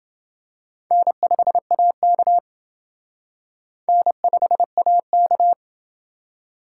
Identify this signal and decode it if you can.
Morse code, heard as a keyed tone, about 28 wpm, 700 Hz. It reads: N5AK N5AK